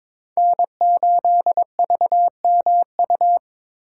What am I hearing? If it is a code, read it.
Morse code, heard as a keyed tone, about 25 wpm, 700 Hz. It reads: N8VMU
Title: Speed Racing - USA 1x3 Call Signs 22wpm